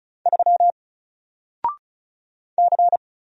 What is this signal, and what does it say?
Morse code, heard as a keyed tone, about 36 wpm, 700 Hz. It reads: 3 C
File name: Single-Letter-Number-35wpm